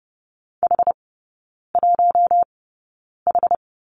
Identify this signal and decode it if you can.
Morse code, heard as a keyed tone, about 30 wpm, spaced as for 9 wpm, 700 Hz. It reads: H1H
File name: Lesson 16e - Letter c - Three Characters 30wpm